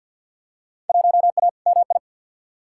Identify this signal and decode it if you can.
Morse code, heard as a keyed tone, about 52 wpm, 700 Hz. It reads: 1A MI
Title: Field Day Exchanges - 50wpm